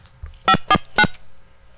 DTMF, heard as telephone touch tones, tones of about 60 ms. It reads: 549